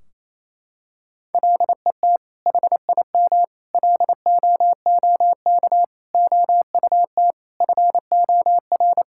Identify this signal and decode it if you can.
Morse code, heard as a keyed tone, about 28 wpm, 700 Hz. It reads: LET HIM LOOK OUT FOR
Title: Sentences from Top 1600 Words 28wpm